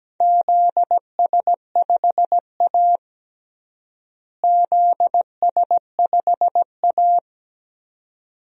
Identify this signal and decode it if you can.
Morse code, heard as a keyed tone, about 17 wpm, 700 Hz. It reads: ZS5A ZS5A
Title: DXCC-100 Call Signs - 4 Characters - Repeated 2x - 17wpm